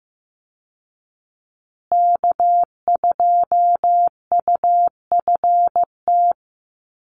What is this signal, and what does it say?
Morse code, heard as a keyed tone, about 15 wpm, 700 Hz. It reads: K2UFT
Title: Call Signs Repeated 3x - 15wpm